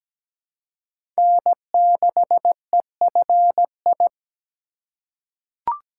Morse code, heard as a keyed tone, about 17 wpm, 700 Hz. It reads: N6EFI
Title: USA Call Signs - 5 Characters - 2x Character Spacing - 17wpm